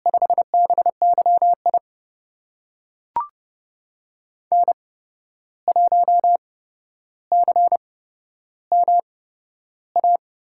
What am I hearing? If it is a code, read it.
Morse code, heard as a keyed tone, about 29 wpm, spaced as for 12 wpm, 700 Hz. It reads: 5BYI N1CMA
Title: USA Call Signs - 5 Characters - 8x Character Spacing - 30wpm